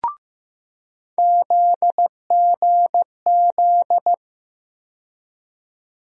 Morse code, heard as a keyed tone, about 15 wpm, 700 Hz. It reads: ZGZ
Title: USA Call Sign Postfixes - 3 Letters 15wpm